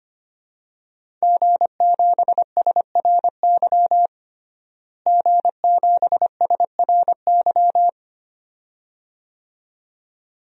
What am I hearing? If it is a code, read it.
Morse code, heard as a keyed tone, about 25 wpm, 700 Hz. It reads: G7SRY G7SRY
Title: DXCC-100 Call Signs - 5 Characters - Repeated 2x - 25wpm